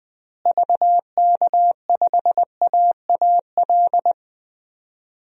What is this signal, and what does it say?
Morse code, heard as a keyed tone, about 21 wpm, 700 Hz. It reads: VK5AAL
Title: DXCC-100 Call Signs - 6 Characters - Repeated 2x - 20wpm